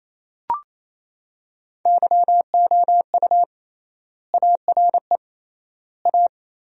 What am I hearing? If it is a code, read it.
Morse code, heard as a keyed tone, about 28 wpm, 700 Hz. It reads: YOU ARE A
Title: 3x Word-Spacing - Sentences from Top 400 Words - 28wpm